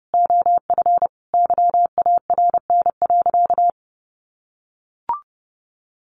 Morse code, heard as a keyed tone, about 30 wpm, 700 Hz. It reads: OF YARN.
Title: Additional Easy Phrases 30wpm